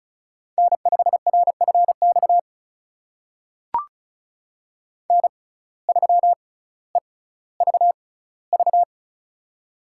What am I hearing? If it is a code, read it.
Morse code, heard as a keyed tone, about 36 wpm, spaced as for 18 wpm, 700 Hz. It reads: N5RFX N3EVV
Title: USA Call Signs - 5 Characters - 6x Character Spacing - 35wpm